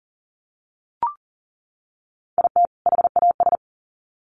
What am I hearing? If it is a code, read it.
Morse code, heard as a keyed tone, about 40 wpm, 700 Hz. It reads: IT HAS
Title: Sets of 2 Words 40wpm